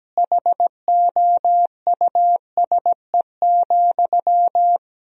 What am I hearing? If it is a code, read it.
Morse code, heard as a keyed tone, about 17 wpm, 700 Hz. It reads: HOUSE,
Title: 4x Word-Spacing - Sentences from Top 400 Words - 17wpm